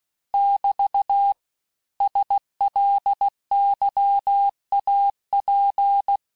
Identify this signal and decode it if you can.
Morse code, heard as a keyed tone, about 16 wpm, 800 Hz. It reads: <BT> SLYAP
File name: cours-du-lundi files / cours-1592--teste-chiffre-vitesse-15mpm